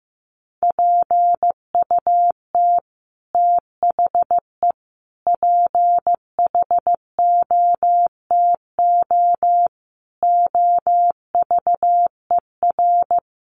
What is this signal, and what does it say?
Morse code, heard as a keyed tone, about 15 wpm, 700 Hz. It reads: PUT THE PHOTO OVER
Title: Sentences from Top 1200 Words 15wpm